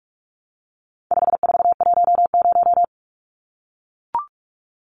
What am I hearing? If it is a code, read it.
Morse code, heard as a keyed tone, about 44 wpm, 700 Hz. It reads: 5410
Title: Numbers - 4-Digit 45wpm